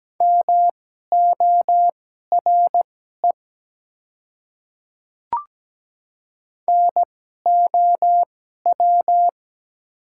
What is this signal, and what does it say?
Morse code, heard as a keyed tone, about 17 wpm, spaced as for 12 wpm, 700 Hz. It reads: MORE NOW
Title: 2x Character Spacing - Top 100 Words - 17wpm